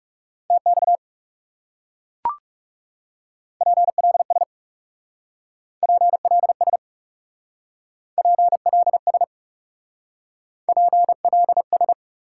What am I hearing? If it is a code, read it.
Morse code, heard as a keyed tone, about 37 wpm, 700 Hz. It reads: TX PLS PLS PLS PLS
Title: Speed-Racing Top 180 QSO Elements 30wpm